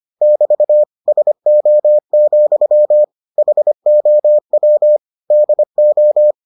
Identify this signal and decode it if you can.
Morse code, heard as a keyed tone, about 25 wpm, 600 Hz. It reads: <BT> SO, HOW DO